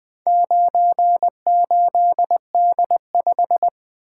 Morse code, heard as a keyed tone, about 20 wpm, 700 Hz. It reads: 98D5
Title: Four Letter-Number Combinations - 20wpm